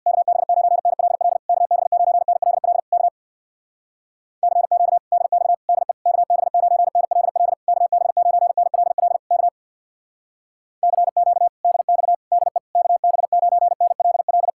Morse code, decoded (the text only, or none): KB9NLL KB9NLL K CQ DX DE KB9NLL KB9NLL K CQ DX DE KB9NLL